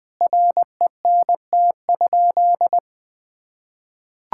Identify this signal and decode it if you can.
Morse code, heard as a keyed tone, about 20 wpm, 700 Hz. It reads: RENT?
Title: Sentences from Top 1000 Words - Review Part 5 - 20wpm